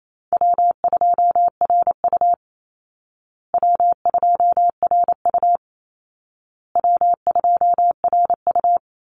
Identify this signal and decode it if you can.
Morse code, heard as a keyed tone, about 28 wpm, 700 Hz. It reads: W2RU W2RU W2RU
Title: Easy Call Signs Repeated 3x - 28wpm